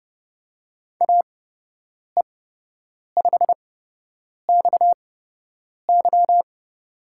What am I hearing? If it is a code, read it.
Morse code, heard as a keyed tone, about 30 wpm, spaced as for 8 wpm, 700 Hz. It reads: AE5XY
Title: USA Call Signs - 5 Characters - 8x Character Spacing - 30wpm